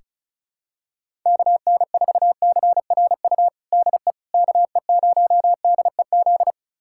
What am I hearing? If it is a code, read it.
Morse code, heard as a keyed tone, about 36 wpm, 700 Hz. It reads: KN4CRU DE KE0DEZ
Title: USA Callsign DE Callsign - 6 Characters - 35wpm